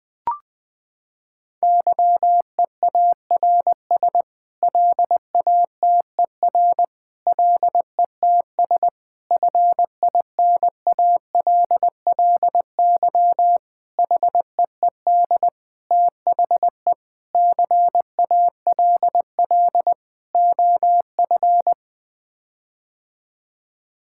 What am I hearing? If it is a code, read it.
Morse code, heard as a keyed tone, about 20 wpm, 700 Hz. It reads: YEARS LATER LETS FINALLY HEED THE CALL OF
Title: Sets of 8 Words - 20wpm